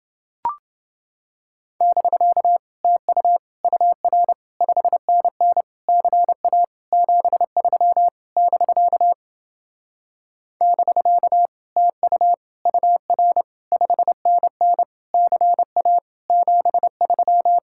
Morse code, heard as a keyed tone, about 29 wpm, 700 Hz. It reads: <BK> TU UR 5NN CA 73 <BK> <BK> TU UR 5NN CA 73